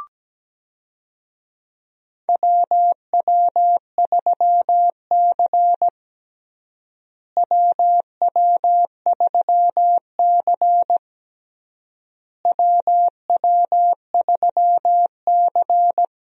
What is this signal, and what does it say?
Morse code, heard as a keyed tone, about 17 wpm, 700 Hz. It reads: WW3C WW3C WW3C
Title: USA Call Signs - 4 Characters - Repeated 3x - 17wpm